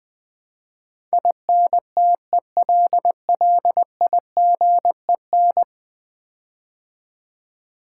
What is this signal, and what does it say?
Morse code, heard as a keyed tone, about 20 wpm, 700 Hz. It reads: INTELLIGEN
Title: Build A Word - 5 Letters At A Time - 12-Letter Words - 20wpm